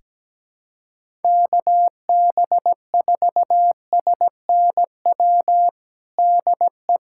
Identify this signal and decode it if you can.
Morse code, heard as a keyed tone, about 17 wpm, 700 Hz. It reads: KB4SNW DE
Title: USA Callsign DE Callsign 17wpm